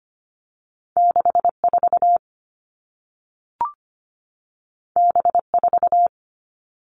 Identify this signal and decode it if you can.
Morse code, heard as a keyed tone, about 25 wpm, 700 Hz. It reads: 6 4 B 4
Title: Two Challenging Letter-Number Combinations 25wpm